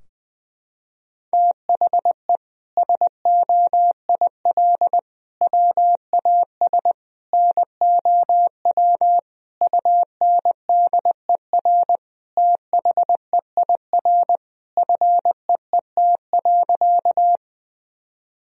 Morse code, heard as a keyed tone, about 20 wpm, 700 Hz. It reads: THE SOIL WAS NOW UNDER THEIR FEET.
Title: Sentences from Top 1000 Words - Review Part 4 - 20wpm